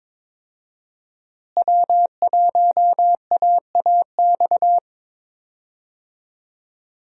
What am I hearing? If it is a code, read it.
Morse code, heard as a keyed tone, about 22 wpm, 700 Hz. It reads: W1AAX
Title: USA Call Signs - 5 Characters - Repeated 2x - 22wpm